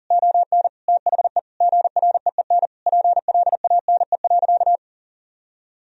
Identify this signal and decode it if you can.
Morse code, heard as a keyed tone, about 40 wpm, 700 Hz. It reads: ON THE GREEN PLANE.
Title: Sentences from Top 1000 Words - Review Part 3 - 40wpm